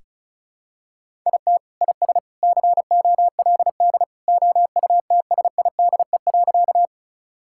Morse code, heard as a keyed tone, about 34 wpm, 700 Hz. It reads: IT IS COLD OUTSIDE.
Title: Sentences from Top 1000 Words - Review Part 6 - 35wpm